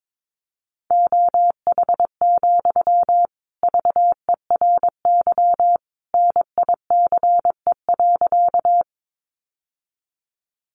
Morse code, heard as a keyed tone, about 22 wpm, 700 Hz. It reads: OH, VERY NICE.
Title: Sentences from Top 1100 Words 22wpm